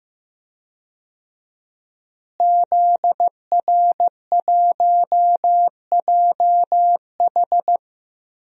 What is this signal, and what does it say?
Morse code, heard as a keyed tone, about 16 wpm, 700 Hz. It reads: ZR1JH
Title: DXCC-100 Call Signs - 5 Characters - Repeated 2x - 15wpm